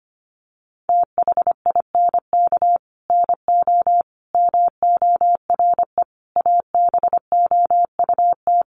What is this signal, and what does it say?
Morse code, heard as a keyed tone, about 25 wpm, 700 Hz. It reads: THINK NO MORE ABOUT
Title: Sentences from Top 200 Words 25wpm